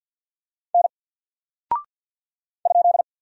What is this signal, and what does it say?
Morse code, heard as a keyed tone, about 48 wpm, 700 Hz. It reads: N ?
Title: Single Character (Letter:Number:Punctuation:Prosign) Alt 50wpm